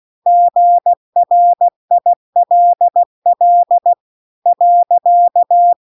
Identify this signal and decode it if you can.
Morse code, heard as a keyed tone, about 16 wpm, 700 Hz. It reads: GRILL .